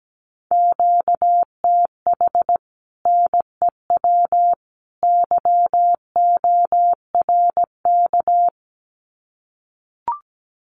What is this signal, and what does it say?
Morse code, heard as a keyed tone, about 17 wpm, 700 Hz. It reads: QTH NEW YORK
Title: Common QSO Phrases 17wpm